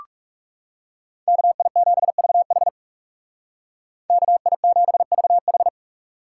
Morse code, heard as a keyed tone, about 40 wpm, 700 Hz. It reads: KI7VH KI7VH